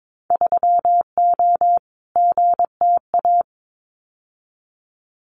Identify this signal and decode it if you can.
Morse code, heard as a keyed tone, about 22 wpm, 700 Hz. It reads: 3O GTA
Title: Winter Field Day Exchanges Encore 22wpm